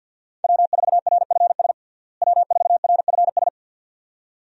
Morse code, heard as a keyed tone, about 51 wpm, 700 Hz. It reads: W4RFS W4RFS
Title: USA Call Signs - 5 Characters - Repeated 2x - 50wpm